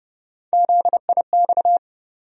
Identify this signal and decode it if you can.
Morse code, heard as a keyed tone, about 30 wpm, 700 Hz. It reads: ZIX
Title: USA Call Sign Postfixes - 3 Letters 30wpm